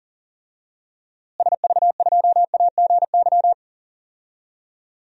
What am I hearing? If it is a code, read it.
Morse code, heard as a keyed tone, about 40 wpm, 700 Hz. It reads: SV2AGY
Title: DXCC-100 Call Signs - 6 Characters - Repeated 2x - 40wpm